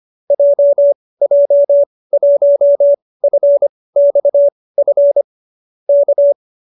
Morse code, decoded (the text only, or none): JJ1FXF K